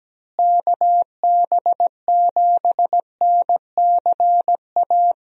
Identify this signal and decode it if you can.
Morse code, heard as a keyed tone, about 17 wpm, 700 Hz. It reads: KB7NCA
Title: USA Call Signs - 6 Characters - Repeated 2x - 17wpm